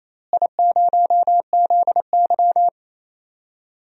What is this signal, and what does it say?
Morse code, heard as a keyed tone, about 28 wpm, 700 Hz. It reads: I0ZY
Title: DXCC-100 Call Signs - 4 Characters - Repeated 2x - 28wpm